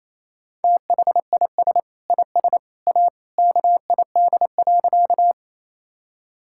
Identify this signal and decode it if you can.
Morse code, heard as a keyed tone, about 28 wpm, 700 Hz. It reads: THIS IS A KID.
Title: Sentences from Top 1600 Words 28wpm